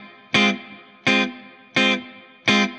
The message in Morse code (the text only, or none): none